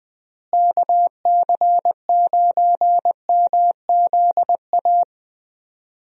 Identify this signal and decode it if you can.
Morse code, heard as a keyed tone, about 20 wpm, 700 Hz. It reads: KC9MZA
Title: USA Call Signs - 6 Characters - Repeated 3x - 20wpm